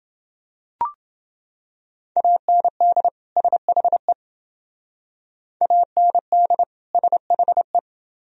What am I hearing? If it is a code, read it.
Morse code, heard as a keyed tone, about 28 wpm, 700 Hz. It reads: AND SHE AND SHE